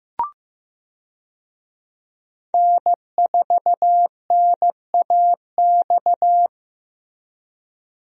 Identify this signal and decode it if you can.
Morse code, heard as a keyed tone, about 16 wpm, 700 Hz. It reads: N4NAX
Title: USA Call Signs - 5 Characters - Repeated 3x - 15wpm